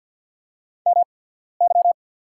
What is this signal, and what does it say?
Morse code, heard as a keyed tone, about 48 wpm, spaced as for 14 wpm, 700 Hz. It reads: MY
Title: 8x Character Spacing - Top 100 Words - 50wpm